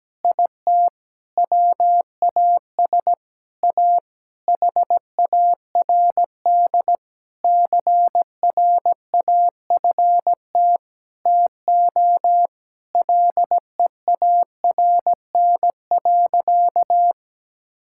Morse code, heard as a keyed tone, about 17 wpm, 700 Hz. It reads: IT WAS A HARD CRAFT TO LEARN.
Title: Sentences from Top 1000 Words - Review Part 6 - 17wpm